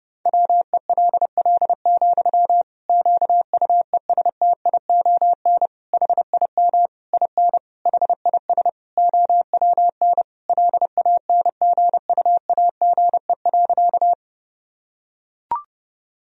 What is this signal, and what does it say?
Morse code, heard as a keyed tone, about 32 wpm, 700 Hz. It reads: WELL, QUESTION HIM IN HIS OWN LANGUAGE.
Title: Sentences from Top 1000 Words - Review Part 3 - 30wpm